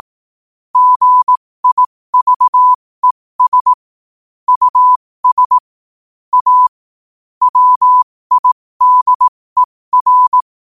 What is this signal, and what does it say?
Morse code, heard as a keyed tone, about 18 wpm, spaced as for 15 wpm, 1000 Hz. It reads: GIVES US A WIDER
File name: qotdfiles / 15wpm